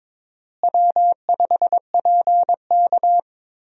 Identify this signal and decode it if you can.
Morse code, heard as a keyed tone, about 22 wpm, 700 Hz. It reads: W5PK
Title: USA Call Signs - 4 Characters - Repeated 3x - 22wpm